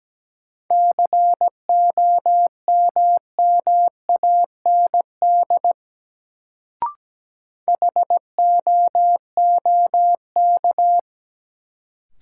Morse code, heard as a keyed tone, about 17 wpm, 700 Hz. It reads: COMMAND HOOK
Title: Sentences from Top 1600 Words - New Word Drill Alt - 17wpm